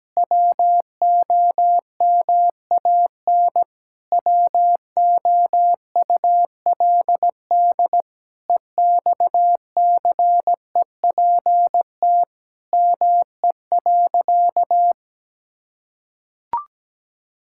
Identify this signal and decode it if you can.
Morse code, heard as a keyed tone, about 17 wpm, 700 Hz. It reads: WOMAN WOULD EXCEPT ME.
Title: Sentences from Top 1600 Words 17wpm